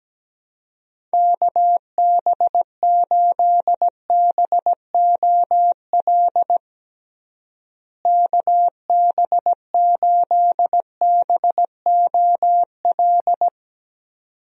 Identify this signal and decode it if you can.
Morse code, heard as a keyed tone, about 17 wpm, 700 Hz. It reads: KB8BOL KB8BOL
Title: USA Call Signs - 6 Characters - Repeated 2x - 17wpm